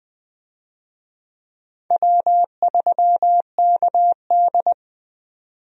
Morse code, heard as a keyed tone, about 20 wpm, 700 Hz. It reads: W3KD
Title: USA Call Signs - 4 Characters - 20wpm